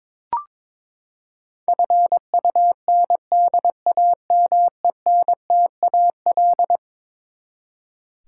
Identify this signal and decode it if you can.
Morse code, heard as a keyed tone, about 22 wpm, 700 Hz. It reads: FUNDAMENTAL
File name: practice / 22wpm / Eleven Letter Words - 22wpm